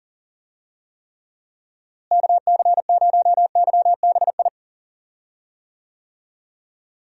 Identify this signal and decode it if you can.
Morse code, heard as a keyed tone, about 40 wpm, 700 Hz. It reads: KC0YBI
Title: USA Call Signs - 6 Characters - Repeated 2x - 40wpm